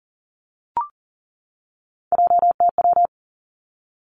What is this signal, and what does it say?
Morse code, heard as a keyed tone, about 37 wpm, 700 Hz. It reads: JTW
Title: Three Challenging Letter-Number Combinations 40wpm